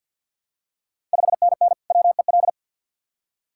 Morse code, heard as a keyed tone, about 52 wpm, 700 Hz. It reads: 5NN WEL
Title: Ontario QSO Party Exchange 50wpm